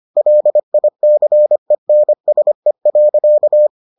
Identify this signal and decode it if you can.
Morse code, heard as a keyed tone, about 25 wpm, 600 Hz. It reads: LICENSE.